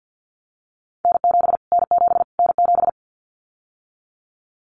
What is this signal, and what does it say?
Morse code, heard as a keyed tone, about 50 wpm, 700 Hz. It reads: N7 N7 N7
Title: Callsign Prefixes for Top 25 DXCC - Repeated 3x - 50wpm